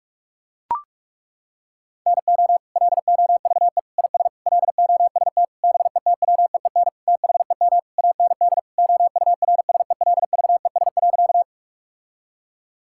Morse code, heard as a keyed tone, about 41 wpm, 700 Hz. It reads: NO LOVE IS LOST BETWEEN THEM AND OURSELVES.
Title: Sentences from Top 1600 Words 45wpm